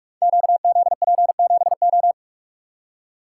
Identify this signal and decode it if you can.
Morse code, heard as a keyed tone, about 44 wpm, 700 Hz. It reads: QZP7O
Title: Code Groups - 5 Letter:Numbers - 45wpm